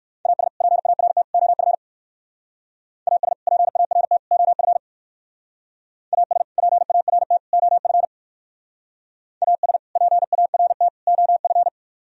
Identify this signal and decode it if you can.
Morse code, heard as a keyed tone, about 62 wpm, 700 Hz. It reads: AS PART OF AS PART OF AS PART OF AS PART OF